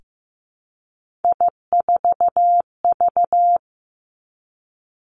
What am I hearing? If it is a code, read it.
Morse code, heard as a keyed tone, about 15 wpm, 700 Hz. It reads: I4V
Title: Three Challenging Letter-Number Combinations 15wpm